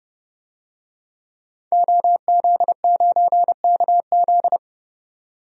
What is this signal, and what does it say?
Morse code, heard as a keyed tone, about 29 wpm, 700 Hz. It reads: OZ9KZ
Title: DXCC-100 Call Signs - 5 Characters - 30wpm